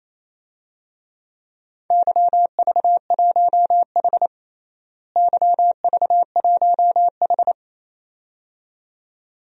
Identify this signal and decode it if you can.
Morse code, heard as a keyed tone, about 28 wpm, 700 Hz. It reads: YV1H YV1H
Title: DXCC-100 Call Signs - 4 Characters - Repeated 2x - 28wpm